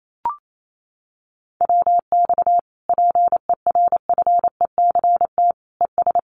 Morse code, heard as a keyed tone, about 28 wpm, 700 Hz. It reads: WX PERFECT ES
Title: Common QSO Phrases Encore 28wpm